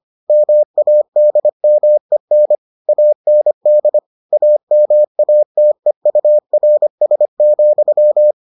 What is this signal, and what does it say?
Morse code, heard as a keyed tone, about 25 wpm, 600 Hz. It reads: MADMEN AND AMATEURS,